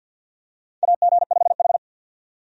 Morse code, heard as a keyed tone, about 48 wpm, 700 Hz. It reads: AG5H
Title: USA Call Signs - 4 Characters - 50wpm